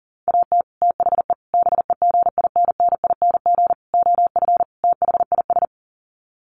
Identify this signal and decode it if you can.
Morse code, heard as a keyed tone, about 40 wpm, 700 Hz. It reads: AT THE BEGINNING OF THIS
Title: Sets of 5 Words Encore 40wpm